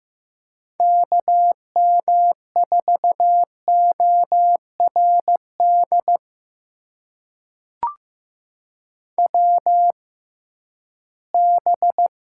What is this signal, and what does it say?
Morse code, heard as a keyed tone, about 15 wpm, spaced as for 11 wpm, 700 Hz. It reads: KM4ORD WB
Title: USA Call Signs - 6 Characters - 6x Character Spacing - 15wpm